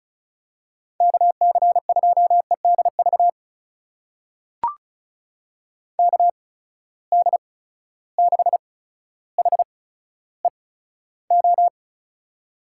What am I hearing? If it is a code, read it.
Morse code, heard as a keyed tone, about 36 wpm, spaced as for 15 wpm, 700 Hz. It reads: KC2EDV KD6HEO
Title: USA Call Signs - 6 Characters - 8x Character Spacing - 35wpm